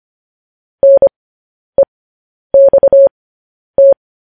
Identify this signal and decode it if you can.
Morse code, heard as a keyed tone, about 25 wpm, spaced as for 10 wpm, 550 Hz. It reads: NEXT
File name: podcast / News headlines-1x-25x10